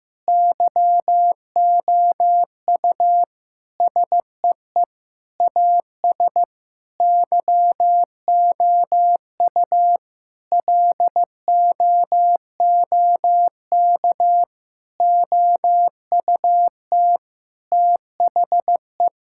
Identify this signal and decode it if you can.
Morse code, heard as a keyed tone, about 16 wpm, 700 Hz. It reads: YOU SEE AS YOU LOOK OUT THE
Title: Sentences from Top 1600 Words 15wpm